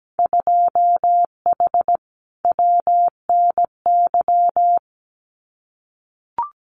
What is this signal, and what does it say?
Morse code, heard as a keyed tone, about 17 wpm, 700 Hz. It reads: 2H WNY
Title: Winter Field Day Exchanges Encore 17wpm